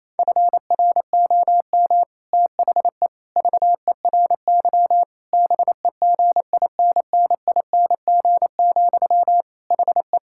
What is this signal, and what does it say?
Morse code, heard as a keyed tone, about 27 wpm, 700 Hz. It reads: FROM THE VERY BEGINNING, HE
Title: Sentences from Top 1600 Words 28wpm